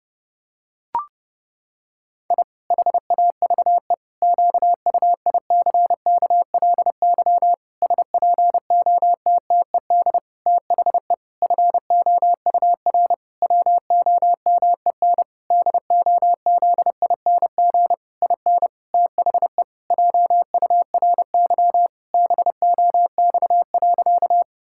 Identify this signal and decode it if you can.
Morse code, heard as a keyed tone, about 30 wpm, 700 Hz. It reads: I HAVE QUICKLY SPOTTED THE FOUR WOMEN DOZING IN THE JURY BOX.
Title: Pangram Challenge v2 - 30wpm